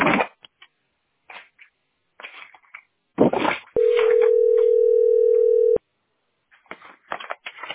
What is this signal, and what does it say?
Ringback tone (North America), 440 + 480 Hz.